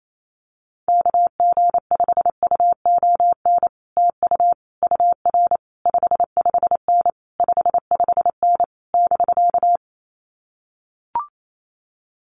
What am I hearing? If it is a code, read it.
Morse code, heard as a keyed tone, about 28 wpm, 700 Hz. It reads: KG5UON TU UR 55N 55N <BK>
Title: Common QSO Phrases Encore 28wpm